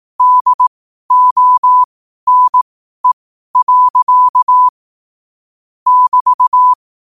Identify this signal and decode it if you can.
Morse code, heard as a keyed tone, about 18 wpm, spaced as for 12 wpm, 1000 Hz. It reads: DONE. <BT>
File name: qotdfiles / 12wpm